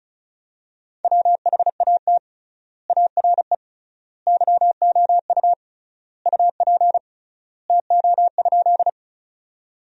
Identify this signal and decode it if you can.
Morse code, heard as a keyed tone, about 36 wpm, 700 Hz. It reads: WHAT ARE YOU UP TO?
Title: 3x Word-Spacing - Sentences from Top 100 Words - 35wpm